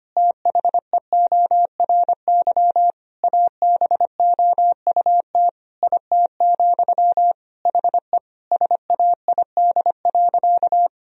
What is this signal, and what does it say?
Morse code, heard as a keyed tone, about 25 wpm, 700 Hz. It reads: THEORY ABOUT IT, HE SAID.
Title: Sentences from Top 1000 Words - Review Part 3 - 25wpm